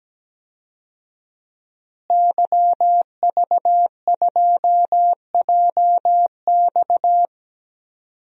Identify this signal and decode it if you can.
Morse code, heard as a keyed tone, about 17 wpm, 700 Hz. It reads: YV2JX
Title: DXCC-100 Call Signs - 5 Characters - Repeated 3x - 17wpm